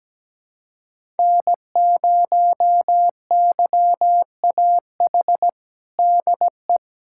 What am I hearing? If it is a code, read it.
Morse code, heard as a keyed tone, about 17 wpm, 700 Hz. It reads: N0YAH DE
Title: USA Callsign DE Callsign - 5 Characters - 17wpm